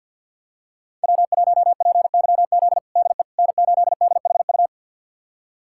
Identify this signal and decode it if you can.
Morse code, heard as a keyed tone, about 46 wpm, 700 Hz. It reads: W1PYZ DE N8DHU
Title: USA Callsign DE Callsign - 5 Characters - 50wpm